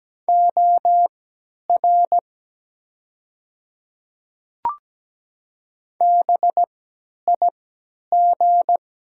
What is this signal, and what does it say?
Morse code, heard as a keyed tone, about 17 wpm, spaced as for 10 wpm, 700 Hz. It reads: OR BIG
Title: 3x Character Spacing - Top 300 Words - 17wpm